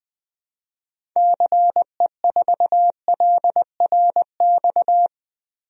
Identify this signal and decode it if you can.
Morse code, heard as a keyed tone, about 21 wpm, 700 Hz. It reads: CE4LRX
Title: DXCC-100 Call Signs - 6 Characters - 20wpm